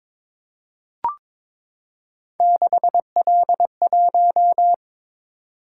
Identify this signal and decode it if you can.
Morse code, heard as a keyed tone, about 22 wpm, 700 Hz. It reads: 6L1
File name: practice / 22wpm / Three Letter-Number Combinations - 22wpm